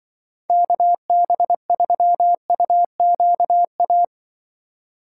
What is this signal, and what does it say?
Morse code, heard as a keyed tone, about 24 wpm, 700 Hz. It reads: KB3UQA
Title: Speed Racing - USA 2x3 Call Signs 20wpm